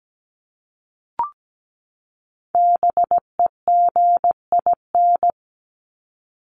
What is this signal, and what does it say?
Morse code, heard as a keyed tone, about 17 wpm, 700 Hz. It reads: BEGIN
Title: ING Suffix - Top 100 Words with Base Words 17wpm